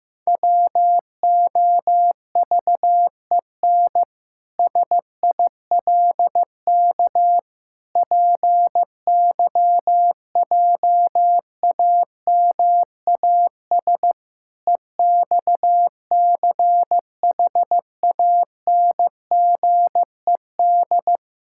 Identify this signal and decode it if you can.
Morse code, heard as a keyed tone, about 16 wpm, 700 Hz. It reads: WOVEN SILK PYJAMAS EXCHANGED
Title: Pangram Challenge v2 - 15wpm